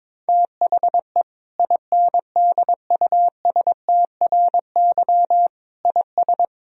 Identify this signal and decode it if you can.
Morse code, heard as a keyed tone, about 22 wpm, 700 Hz. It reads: THE INDUSTRY IS
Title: Sentences from Top 1000 Words - Review Part 4 - 22wpm